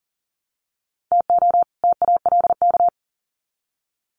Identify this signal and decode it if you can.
Morse code, heard as a keyed tone, about 40 wpm, 700 Hz. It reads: TO TALK
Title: Sets of 2 Words 40wpm